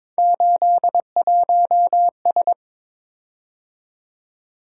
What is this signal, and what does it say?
Morse code, heard as a keyed tone, about 22 wpm, 700 Hz. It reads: 81S
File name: practice / 22wpm / Three Letter-Number Combinations - 22wpm